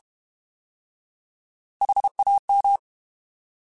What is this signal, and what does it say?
Morse code, heard as a keyed tone, about 32 wpm, 800 Hz. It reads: HAM
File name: Alphabetical / Alphabetical mp3s / Hs @30